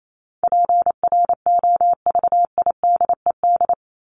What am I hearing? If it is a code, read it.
Morse code, heard as a keyed tone, about 28 wpm, 700 Hz. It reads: PROVIDED
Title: Top 1751-2000 Words 28wpm